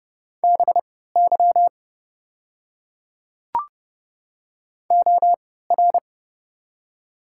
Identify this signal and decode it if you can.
Morse code, heard as a keyed tone, about 31 wpm, spaced as for 17 wpm, 700 Hz. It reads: BY OR
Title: 3x Character Spacing - Top 100 Words - 30wpm